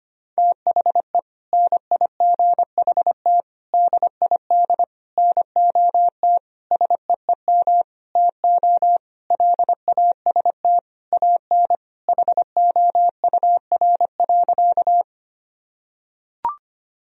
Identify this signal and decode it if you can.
Morse code, heard as a keyed tone, about 25 wpm, 700 Hz. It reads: THE NIGHT DID NOT SEEM TO LAST AN HOUR.
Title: Sentences from Top 1000 Words - Review Part 1 - 25wpm